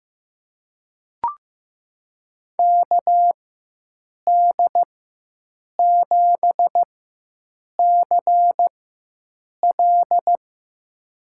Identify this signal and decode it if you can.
Morse code, heard as a keyed tone, about 15 wpm, spaced as for 7 wpm, 700 Hz. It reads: KD7CL
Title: USA Call Signs - 5 Characters - 4x Character Spacing - 15wpm